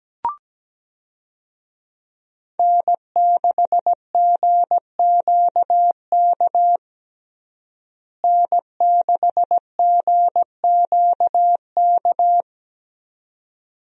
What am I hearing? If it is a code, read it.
Morse code, heard as a keyed tone, about 17 wpm, 700 Hz. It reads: N6GQK N6GQK
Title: USA Call Signs - 5 Characters - Repeated 2x - 17wpm